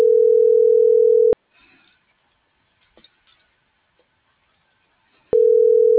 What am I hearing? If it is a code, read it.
Ringback tone (North America), 440 + 480 Hz.
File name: CENTRAL MAYAB / out vianey.prado